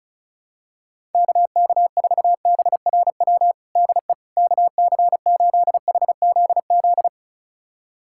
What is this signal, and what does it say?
Morse code, heard as a keyed tone, about 36 wpm, 700 Hz. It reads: KK4BRW DE KC8HZZ
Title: USA Callsign DE Callsign - 6 Characters - 35wpm